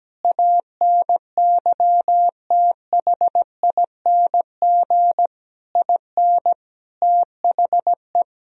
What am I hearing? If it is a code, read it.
Morse code, heard as a keyed tone, about 17 wpm, 700 Hz. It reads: ANYTHING IN THE
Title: Sentences from Top 1000 Words - Review Part 3 - 17wpm